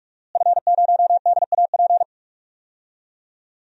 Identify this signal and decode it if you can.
Morse code, heard as a keyed tone, about 44 wpm, 700 Hz. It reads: F0DAP
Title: DXCC-100 Call Signs - 5 Characters - Repeated 2x - 45wpm